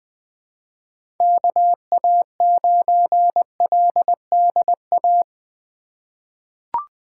Morse code, heard as a keyed tone, about 20 wpm, 700 Hz. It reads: KA9LDA
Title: USA Call Signs - 6 Characters - 2x Character Spacing - 20wpm